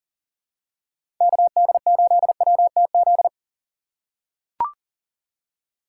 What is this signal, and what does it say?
Morse code, heard as a keyed tone, about 40 wpm, 700 Hz. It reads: KD8WTZ
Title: USA Call Signs - 6 Characters - 6x Character Spacing - 40wpm